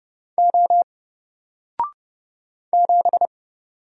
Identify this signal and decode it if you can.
Morse code, heard as a keyed tone, about 30 wpm, 700 Hz. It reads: O 7
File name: Single-Letter-Number-30wpm